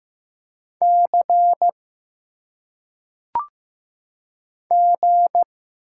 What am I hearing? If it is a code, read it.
Morse code, heard as a keyed tone, about 15 wpm, 700 Hz. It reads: C G
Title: Single Letters - Phonetic Pronunciation 15wpm